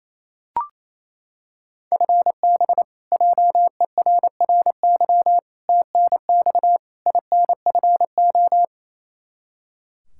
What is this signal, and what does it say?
Morse code, heard as a keyed tone, about 28 wpm, 700 Hz. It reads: FB JERRY TNX INFO
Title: K1USN SST QSO Practice 28wpm